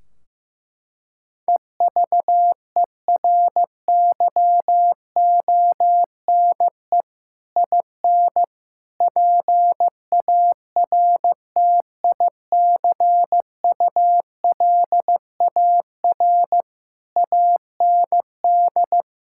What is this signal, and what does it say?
Morse code, heard as a keyed tone, about 15 wpm, 700 Hz. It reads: EVERYONE IN PARTICULAR AND
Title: Sentences from Top 1600 Words 15wpm